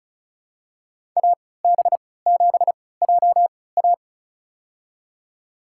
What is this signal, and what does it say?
Morse code, heard as a keyed tone, about 34 wpm, spaced as for 20 wpm, 700 Hz. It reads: AB7JA
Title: USA Call Signs - 5 Characters - 3x Character Spacing - 35wpm